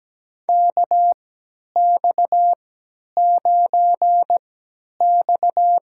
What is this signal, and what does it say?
Morse code, heard as a keyed tone, about 17 wpm, spaced as for 10 wpm, 700 Hz. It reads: KX9X
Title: USA Call Signs - 4 Characters - 3x Character Spacing - 17wpm